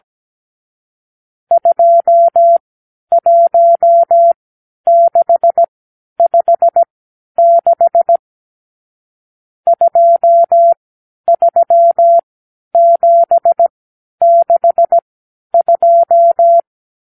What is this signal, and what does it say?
Morse code, heard as a keyed tone, about 17 wpm, spaced as for 10 wpm, 680 Hz. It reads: 21656 23762
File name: k7qo / Lesson 54 (9)